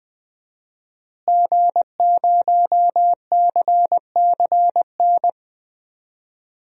Morse code, heard as a keyed tone, about 20 wpm, 700 Hz. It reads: G0CCN
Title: DXCC-100 Call Signs - 5 Characters - Repeated 3x - 20wpm